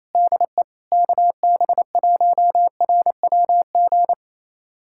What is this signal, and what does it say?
Morse code, heard as a keyed tone, about 28 wpm, 700 Hz. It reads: DE KB1RWG
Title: USA Callsign DE Callsign - 6 Characters - 28wpm